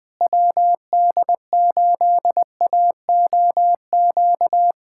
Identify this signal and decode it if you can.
Morse code, heard as a keyed tone, about 20 wpm, 700 Hz. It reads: WD8AOQ
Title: USA Callsign DE Callsign - 6 Characters - 20wpm